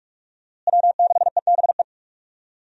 Morse code, heard as a keyed tone, about 43 wpm, 700 Hz. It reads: W6EBE
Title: USA Call Signs - 5 Characters - Repeated 2x - 45wpm